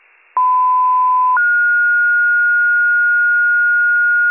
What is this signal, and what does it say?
Two-tone page, heard as a one-second tone, then a three-second tone, about 1010 then 1430 Hz. It is heard from 0.4 to 4.3 s.